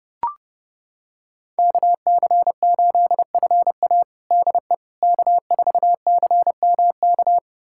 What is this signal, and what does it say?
Morse code, heard as a keyed tone, about 31 wpm, 700 Hz. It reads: KC8FA DE K4CMK
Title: USA Callsign DE Callsign - 5 Characters - 30wpm